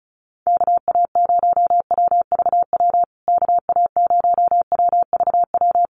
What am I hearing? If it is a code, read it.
Morse code, heard as a keyed tone, about 34 wpm, 700 Hz. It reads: KA0WVW KA0WVW
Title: USA Call Signs Repeated 3x 35wpm